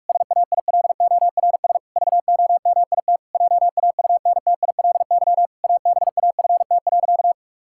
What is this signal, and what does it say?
Morse code, heard as a keyed tone, about 41 wpm, 700 Hz. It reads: SAILORS VOMIT JAUNTILY ABAFT.
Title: Pangram Challenge v2 - 45wpm